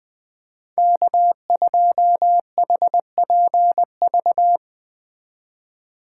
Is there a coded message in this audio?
Yes: Morse code from 0.8 to 4.6 s, heard as a keyed tone, about 20 wpm, 700 Hz. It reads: K2HPV